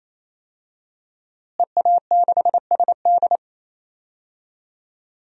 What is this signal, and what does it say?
Morse code, heard as a keyed tone, about 28 wpm, 700 Hz. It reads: EA6SD
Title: DXCC-100 Call Signs - 5 Characters - Repeated 3x - 28wpm